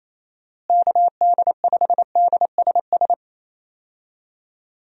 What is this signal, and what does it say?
Morse code, heard as a keyed tone, about 28 wpm, 700 Hz. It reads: KD5DSS
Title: USA Call Signs - 6 Characters - Repeated 2x - 28wpm